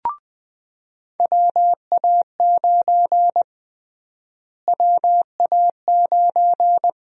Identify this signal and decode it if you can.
Morse code, heard as a keyed tone, about 21 wpm, 700 Hz. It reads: WA9 WA9
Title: USA Call Sign Prefixes - Repeated 2x - 20wpm